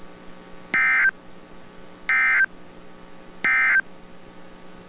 Emergency Alert System (SAME) tones, heard as harsh buzzing bursts. It is heard from 0.6 to 3.9 s.